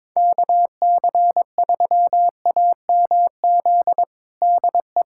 Morse code, heard as a keyed tone, about 22 wpm, 700 Hz. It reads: KC3AMZ DE
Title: USA Callsign DE Callsign - 6 Characters - 22wpm